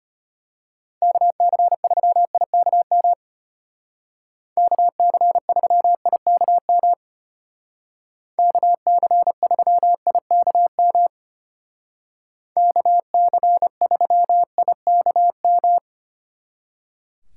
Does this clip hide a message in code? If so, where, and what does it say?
Yes: Morse code from 1.0 to 15.8 s, heard as a keyed tone, about 32 wpm, 700 Hz. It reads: KC3IKM KC3IKM KC3IKM KC3IKM